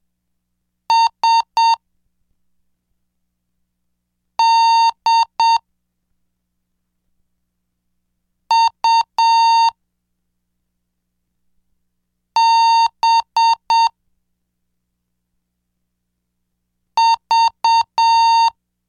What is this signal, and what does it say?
Morse code, heard as a keyed tone, about 7 wpm, 920 Hz. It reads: S D U B V
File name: cursos / CDestudo / 10 Track 10